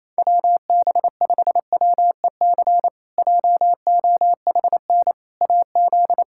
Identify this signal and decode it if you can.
Morse code, heard as a keyed tone, about 28 wpm, 700 Hz. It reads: WB5WEC JOHN AZ
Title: K1USN SST QSO Practice - 28wpm